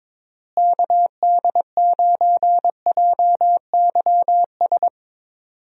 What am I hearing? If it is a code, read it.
Morse code, heard as a keyed tone, about 22 wpm, 700 Hz. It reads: KD9JYS
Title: USA Call Signs - 6 Characters - Repeated 3x - 22wpm